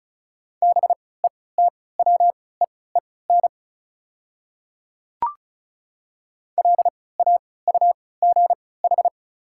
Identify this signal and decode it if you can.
Morse code, heard as a keyed tone, about 36 wpm, spaced as for 20 wpm, 690 Hz. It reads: BETWEEN LAUGH
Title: 3x Character Spacing - Top 500 Words - 35wpm